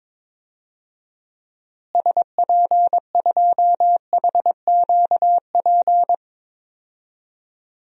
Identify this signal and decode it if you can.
Morse code, heard as a keyed tone, about 22 wpm, 700 Hz. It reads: SP2HQP
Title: DXCC-100 Call Signs - 6 Characters - Repeated 3x - 22wpm